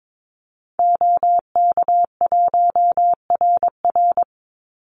Morse code, heard as a keyed tone, about 22 wpm, 700 Hz. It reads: OK1RR
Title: Call Signs - 22wpm